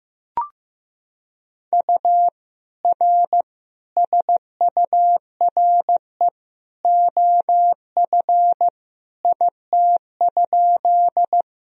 Morse code, heard as a keyed tone, about 16 wpm, 700 Hz. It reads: U R SURE OF IT?
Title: Sentences from Top 300 Words and Abbreviations 15wpm